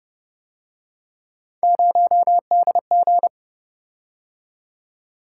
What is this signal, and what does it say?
Morse code, heard as a keyed tone, about 29 wpm, 700 Hz. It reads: 0DG